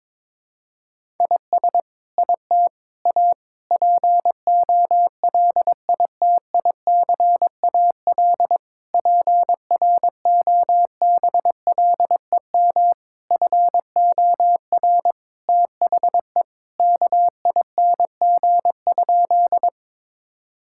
Morse code, heard as a keyed tone, about 22 wpm, 700 Hz. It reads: IS IT A POLITICAL PROBLEM FOR THE KING?
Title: Sentences from Top 1000 Words - Review Part 5 - 22wpm